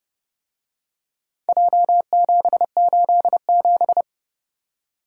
Morse code, heard as a keyed tone, about 28 wpm, 700 Hz. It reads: J787